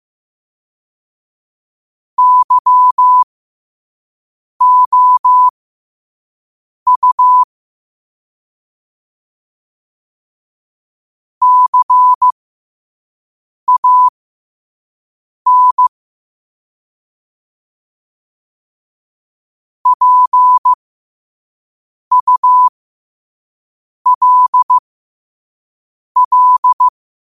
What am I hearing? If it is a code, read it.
Morse code, heard as a keyed tone, about 15 wpm, spaced as for 5 wpm, 1000 Hz. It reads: YOU CAN PULL
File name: qotdfiles / 5wpm